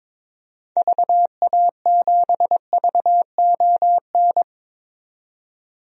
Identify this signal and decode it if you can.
Morse code, heard as a keyed tone, about 21 wpm, 700 Hz. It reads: VA7VON
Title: DXCC-100 Call Signs - 6 Characters - 22wpm